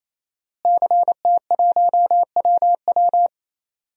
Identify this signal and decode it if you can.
Morse code, heard as a keyed tone, about 28 wpm, 700 Hz. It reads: CT1WW
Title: DXCC-100 Call Signs - 5 Characters - Repeated 3x - 28wpm